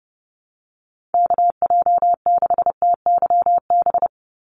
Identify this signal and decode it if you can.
Morse code, heard as a keyed tone, about 30 wpm, 700 Hz. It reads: KJ6TYB
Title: USA Call Signs 30wpm